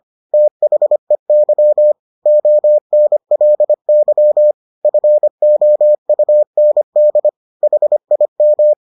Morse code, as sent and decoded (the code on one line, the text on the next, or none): - .... -.-- / --- -. .-.. -.-- / ..-. --- ..- -. -.. / .... .. --
THEY ONLY FOUND HIM